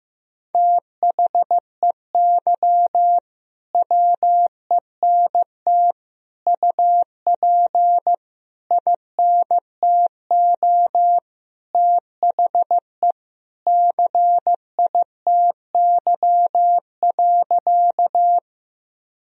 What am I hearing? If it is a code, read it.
Morse code, heard as a keyed tone, about 15 wpm, 700 Hz. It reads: THEY WENT UP INTO THE CITY.
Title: Sentences from Top 1000 Words - Review Part 1 - 15wpm